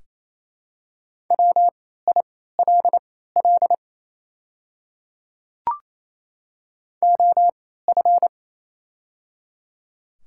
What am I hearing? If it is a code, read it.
Morse code, heard as a keyed tone, about 28 wpm, spaced as for 16 wpm, 700 Hz. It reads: WILL OF
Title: 3x Character Spacing - Top 100 Words - 28wpm